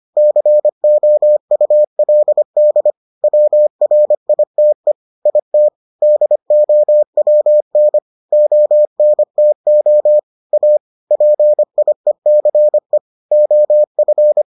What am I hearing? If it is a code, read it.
Morse code, heard as a keyed tone, about 25 wpm, 600 Hz. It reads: COULD WRITE IT DOWN ONTO A PIECE OF